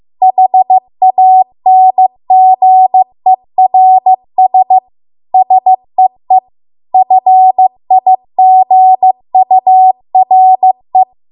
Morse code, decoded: HANGERS SEE FIGURE

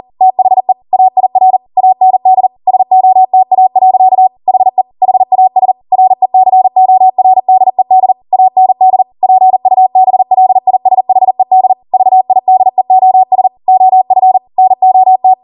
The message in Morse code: THE AIR AND SOTA. HE HAS RECORDED AND PUBLISHED VIDEOS OF NOT

